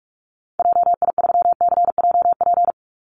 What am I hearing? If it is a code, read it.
Morse code, heard as a keyed tone, about 44 wpm, 700 Hz. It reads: JI3CJP
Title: Call Signs Repeated 3x - 45wpm